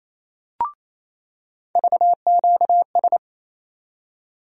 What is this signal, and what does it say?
Morse code, heard as a keyed tone, about 28 wpm, 700 Hz. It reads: VQS
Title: USA Call Sign Postfixes - 3 Letters 28wpm